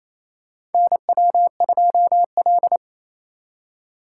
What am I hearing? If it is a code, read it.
Morse code, heard as a keyed tone, about 28 wpm, 700 Hz. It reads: NW2L
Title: USA Call Signs - 3x Character Spacing - 28wpm